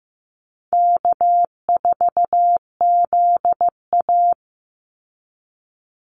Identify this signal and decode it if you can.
Morse code, heard as a keyed tone, about 15 wpm, 700 Hz. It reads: K4ZA
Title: Easy Call Signs Repeated 2x - 15wpm